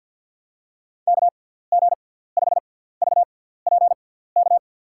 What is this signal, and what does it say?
Morse code, heard as a keyed tone, about 52 wpm, spaced as for 17 wpm, 700 Hz. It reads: KG5VPK
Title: USA Call Signs - 6 Characters - 6x Character Spacing - 50wpm